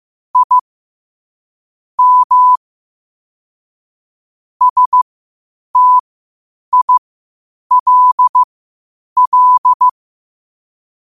Morse code, heard as a keyed tone, about 15 wpm, spaced as for 8 wpm, 1000 Hz. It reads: I M STILL
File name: 8wpm